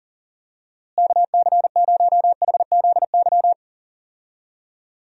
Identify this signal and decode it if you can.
Morse code, heard as a keyed tone, about 40 wpm, 700 Hz. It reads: KC0HZY
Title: USA Call Signs - 6 Characters - Repeated 2x - 40wpm